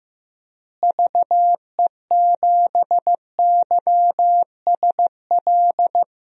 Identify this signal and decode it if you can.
Morse code, heard as a keyed tone, about 15 wpm, 700 Hz. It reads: VE7YSL
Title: DXCC-100 Call Signs - 6 Characters - Repeated 2x - 15wpm